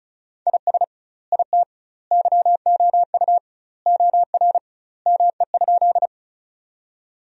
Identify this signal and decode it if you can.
Morse code, heard as a keyed tone, about 34 wpm, 700 Hz. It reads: IS IT YOU OR ME?
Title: 2x Word-Spacing - Sentences from Top 200 Words - 35wpm